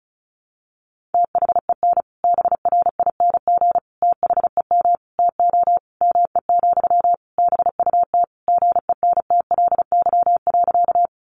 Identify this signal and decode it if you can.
Morse code, heard as a keyed tone, about 34 wpm, 700 Hz. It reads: THEN BRING THEM TO ME, BUT GENTLY.
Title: Sentences from Top 1500 Words 35wpm